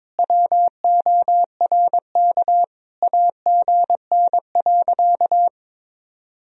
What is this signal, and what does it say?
Morse code, heard as a keyed tone, about 22 wpm, 700 Hz. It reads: WORK AGN.
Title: Sentences from Top 200 Words and Abbreviations 22wpm